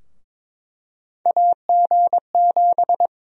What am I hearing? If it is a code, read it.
Morse code, heard as a keyed tone, about 22 wpm, 700 Hz. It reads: AG7
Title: USA Call Sign Prefixes - 22wpm